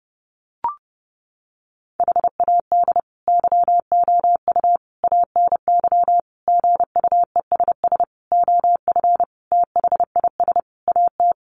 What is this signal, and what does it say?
Morse code, heard as a keyed tone, about 30 wpm, 700 Hz. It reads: HAD YOU ANY GUESS OF THIS AT
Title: Sentences from Top 1500 Words 30wpm